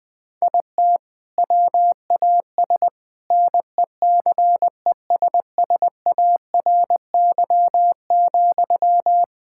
Morse code, heard as a keyed tone, about 20 wpm, 700 Hz. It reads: IT WAS NECESSARY,
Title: Sentences from Top 1600 Words 20wpm